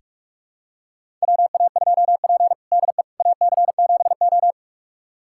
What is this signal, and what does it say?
Morse code, heard as a keyed tone, about 42 wpm, 690 Hz. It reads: WA2P DE AC7O
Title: USA Callsign DE Callsign - 4 Characters - 45wpm